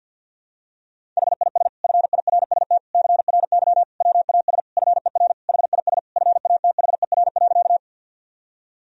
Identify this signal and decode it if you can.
Morse code, heard as a keyed tone, about 52 wpm, 700 Hz. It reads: HIS FIRST CRY WAS FER HIS FATHER.
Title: Sentences from Top 500 Words and Abbreviations 50wpm